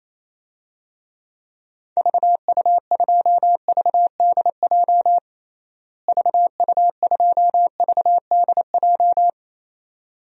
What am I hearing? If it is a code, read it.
Morse code, heard as a keyed tone, about 27 wpm, 700 Hz. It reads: VU2VDJ VU2VDJ
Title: DXCC-100 Call Signs - 6 Characters - Repeated 2x - 28wpm